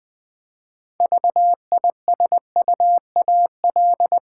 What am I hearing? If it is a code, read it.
Morse code, heard as a keyed tone, about 20 wpm, 700 Hz. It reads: VISUAL
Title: Six Letter Words - 20wpm